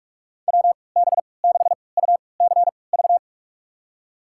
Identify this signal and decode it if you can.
Morse code, heard as a keyed tone, about 42 wpm, spaced as for 25 wpm, 700 Hz. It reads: WB6UCV
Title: USA Call Signs - 6 Characters - 3x Character Spacing - 45wpm